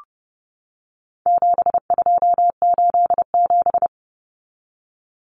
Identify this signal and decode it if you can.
Morse code, heard as a keyed tone, about 30 wpm, 700 Hz. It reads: 7287
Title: Common Cut Numbers - 4-Digit 30wpm